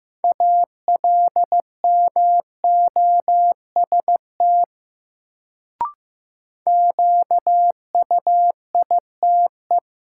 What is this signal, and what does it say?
Morse code, heard as a keyed tone, about 15 wpm, 700 Hz. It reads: ALMOST QUITE
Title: Sentences from Top 700 Words - New Word Drill - 15wpm